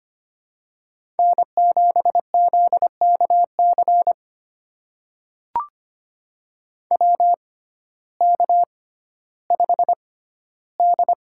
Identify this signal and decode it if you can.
Morse code, heard as a keyed tone, about 25 wpm, spaced as for 14 wpm, 700 Hz. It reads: N7ZKC WK5D
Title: USA Call Signs - 6x Character Spacing - 25wpm